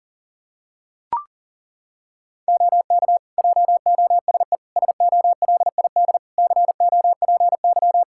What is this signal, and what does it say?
Morse code, heard as a keyed tone, about 40 wpm, 700 Hz. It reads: OK JOSE SOLID COPY
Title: K1USN SST QSO Practice 40wpm